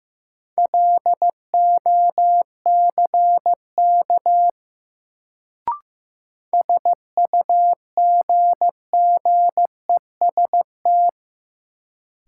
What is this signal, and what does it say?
Morse code, heard as a keyed tone, about 15 wpm, 700 Hz. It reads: LOCK SUGGEST
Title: Sentences from Top 1300 Words - New Word Drill Alt - 15wpm